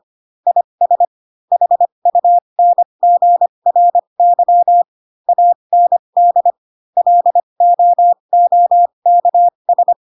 Morse code, heard as a keyed tone, about 25 wpm, spaced as for 22 wpm, 700 Hz. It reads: IS HUNGRY AND LOOKS